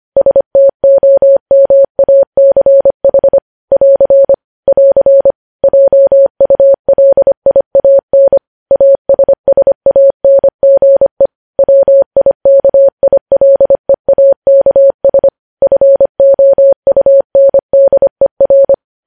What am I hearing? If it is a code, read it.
Morse code, heard as a keyed tone, about 25 wpm, 550 Hz. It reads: STOMACH <AR> <AR> JULIAN ASSANGE WIKILEAKS FOUNDER
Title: News Headlines at 25